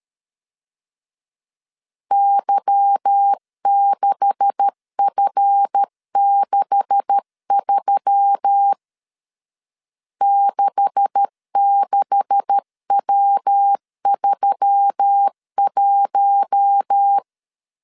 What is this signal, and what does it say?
Morse code, heard as a keyed tone, about 13 wpm, 800 Hz. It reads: Y6F63 66W31
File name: LEZIONI / 9 286 12wpm 2305